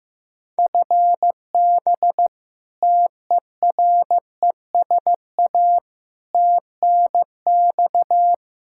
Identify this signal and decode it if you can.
Morse code, heard as a keyed tone, about 15 wpm, 700 Hz. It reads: FB TERESA TNX
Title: Practice QSO - Exchange 3 - More Info - 15wpm